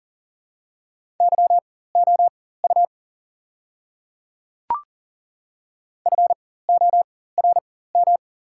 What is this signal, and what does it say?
Morse code, heard as a keyed tone, about 40 wpm, spaced as for 19 wpm, 690 Hz. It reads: YOU FORM
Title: 4x Character Spacing - Top 500 Words - 40wpm